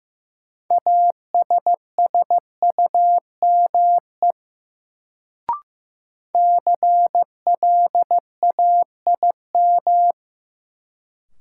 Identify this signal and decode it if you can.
Morse code, heard as a keyed tone, about 15 wpm, 700 Hz. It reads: ASSUME CLAIM
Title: Sentences from Top 1500 Words - New Word Drill - 15wpm